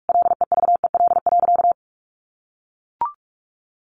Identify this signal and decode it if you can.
Morse code, heard as a keyed tone, about 44 wpm, 700 Hz. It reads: LEVEL.
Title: Sentences from Top 700 Words 45wpm